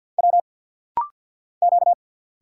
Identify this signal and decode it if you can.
Morse code, heard as a keyed tone, about 49 wpm, 700 Hz. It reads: W Q
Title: Single Letter-Number Alt - Rapid-Fire 50wpm